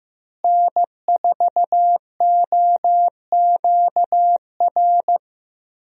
Morse code, heard as a keyed tone, about 16 wpm, 700 Hz. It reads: N4OQR
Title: USA Call Signs - 5 Characters - Repeated 2x - 15wpm